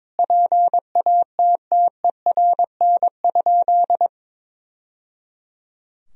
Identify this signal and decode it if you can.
Morse code, heard as a keyed tone, about 21 wpm, 700 Hz. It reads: PATTERN?
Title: Sentences from Top 1000 Words - Review Part 2 - 22wpm